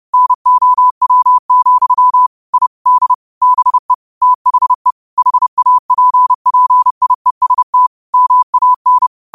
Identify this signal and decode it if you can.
Morse code, heard as a keyed tone, about 30 wpm, 1000 Hz. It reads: NOW, I D BE THE HAPPIEST MAN
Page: 30 WPM Morse code quotes